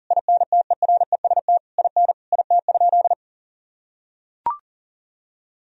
Morse code, heard as a keyed tone, about 40 wpm, 700 Hz. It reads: INTEREST IN IT?
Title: Sentences from Top 1000 Words - Review Part 4 - 40wpm